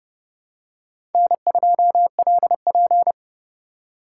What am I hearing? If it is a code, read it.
Morse code, heard as a keyed tone, about 29 wpm, 700 Hz. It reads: N2LP